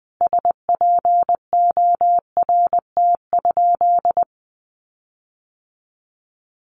Morse code, heard as a keyed tone, about 20 wpm, 700 Hz. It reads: SPORT?
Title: Sentences from Top 1500 Words 20wpm